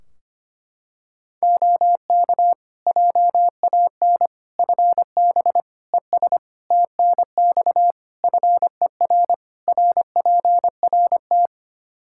Morse code, heard as a keyed tone, about 25 wpm, 700 Hz. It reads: OK JAN FB ES TNX FER RPRT
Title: K1USN SST QSO Practice 25wpm